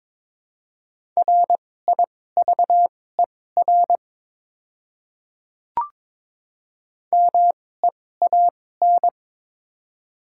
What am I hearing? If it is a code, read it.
Morse code, heard as a keyed tone, about 23 wpm, spaced as for 16 wpm, 700 Hz. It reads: RIVER MEAN
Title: 2x Character Spacing - Top 500 Words - 22wpm